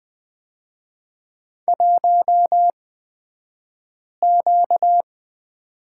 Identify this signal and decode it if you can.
Morse code, heard as a keyed tone, about 20 wpm, spaced as for 5 wpm, 690 Hz. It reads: 1Q
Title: 235 Lesson 38d - Letter q - Two Characters 20wpm